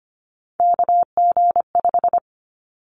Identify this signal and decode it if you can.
Morse code, heard as a keyed tone, about 25 wpm, 700 Hz. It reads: KG5
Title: Callsign Prefixes for Top 25 DXCC - 25wpm